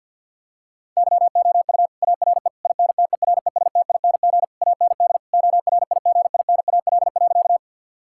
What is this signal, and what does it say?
Morse code, heard as a keyed tone, about 52 wpm, 700 Hz. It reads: YOU ARE INTERESTING AND ORIGINAL.
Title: Sentences from Top 1000 Words - Review Part 6 - 50wpm